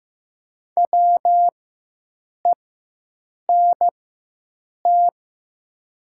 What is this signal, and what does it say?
Morse code, heard as a keyed tone, about 15 wpm, spaced as for 7 wpm, 700 Hz. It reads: WENT
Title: 4x Character Spacing - Top 200 Words - 15wpm